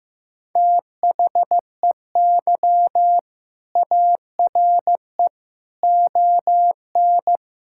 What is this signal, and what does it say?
Morse code, heard as a keyed tone, about 15 wpm, 700 Hz. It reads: THEY ARE ON
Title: Sentences from Top 1600 Words 15wpm